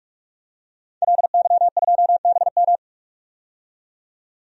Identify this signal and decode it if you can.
Morse code, heard as a keyed tone, about 47 wpm, 700 Hz. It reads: LY2BM
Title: DXCC-100 Call Signs - 5 Characters - 45wpm